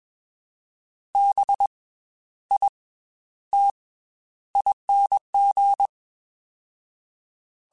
Morse code, heard as a keyed tone, about 21 wpm, spaced as for 11 wpm, 800 Hz. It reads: BITING